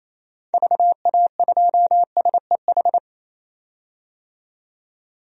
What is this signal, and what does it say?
Morse code, heard as a keyed tone, about 28 wpm, 700 Hz. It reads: VA2SEH
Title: DXCC-100 Call Signs - 6 Characters - 28wpm